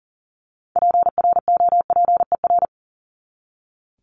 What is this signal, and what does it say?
Morse code, heard as a keyed tone, about 40 wpm, 700 Hz. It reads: PROPER
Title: Top 751-1000 Words 40wpm